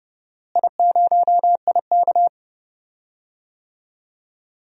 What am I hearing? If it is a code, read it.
Morse code, heard as a keyed tone, about 30 wpm, 700 Hz. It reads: I0IK
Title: DXCC-100 Call Signs - 4 Characters - Repeated 3x - 30wpm